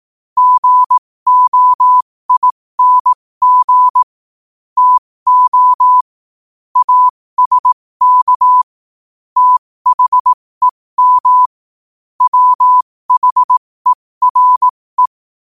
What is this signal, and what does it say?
Morse code, heard as a keyed tone, about 18 wpm, spaced as for 15 wpm, 1000 Hz. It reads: GOING TO ASK THEM WHERE
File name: qotdfiles / 15wpm